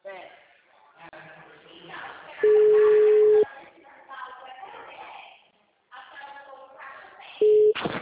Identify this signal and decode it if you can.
Ringback tone (Europe, 425 Hz), with speech.